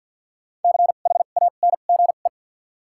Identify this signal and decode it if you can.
Morse code, heard as a keyed tone, about 46 wpm, 700 Hz. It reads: C H A N G E
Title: 2x Character Spacing - Top 200 Words - 50wpm